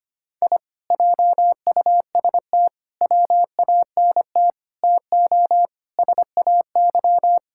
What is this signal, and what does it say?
Morse code, heard as a keyed tone, about 25 wpm, 700 Hz. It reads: I JUST WANT TO SAY
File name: practice / 25wpm / Sentences with 2-Word Phrases - 25wpm